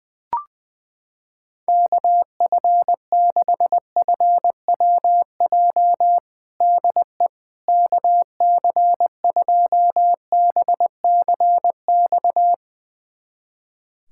Morse code, heard as a keyed tone, about 20 wpm, 700 Hz. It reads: KF6FWJ DE KC2BCX
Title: USA Callsign DE Callsign - 6 Characters - 20wpm